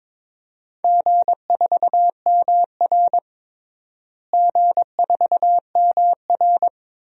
Morse code, heard as a keyed tone, about 22 wpm, 700 Hz. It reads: G4MR G4MR
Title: DXCC-100 Call Signs - 4 Characters - Repeated 3x - 22wpm